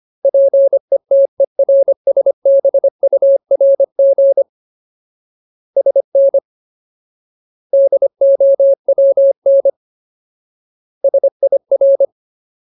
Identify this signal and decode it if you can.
Morse code, heard as a keyed tone, about 25 wpm, 550 Hz. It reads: PETERSBURG SN DOWN SIR
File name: Ultimate Word List / 25WPM Ultimate